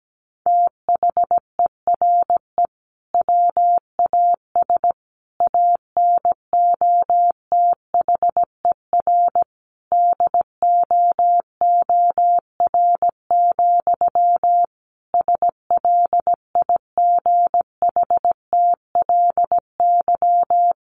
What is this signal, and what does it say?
Morse code, heard as a keyed tone, about 17 wpm, 700 Hz. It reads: THERE WAS ANOTHER DOOR, SLIGHTLY
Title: Sentences from Top 1300 Words 17wpm